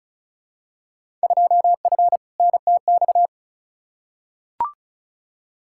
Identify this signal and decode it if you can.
Morse code, heard as a keyed tone, about 36 wpm, 700 Hz. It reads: 2F NTX
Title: Field Day Exchanges - 35wpm